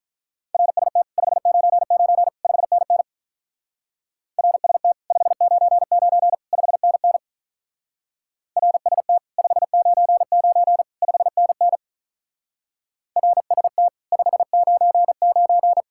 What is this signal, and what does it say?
Morse code, heard as a keyed tone, about 42 wpm, 700 Hz. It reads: RST 599 5NN RST 599 5NN RST 599 5NN RST 599
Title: Speed-Racing Common QSO Phrases - 35wpm